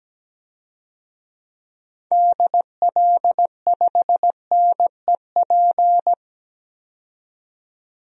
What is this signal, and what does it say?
Morse code, heard as a keyed tone, about 17 wpm, 700 Hz. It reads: DL5NEP
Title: DXCC-100 Call Signs - 6 Characters - 17wpm